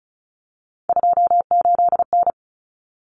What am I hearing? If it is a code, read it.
Morse code, heard as a keyed tone, about 34 wpm, 700 Hz. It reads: 28N
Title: Common Cut Numbers - 3-Digit 35wpm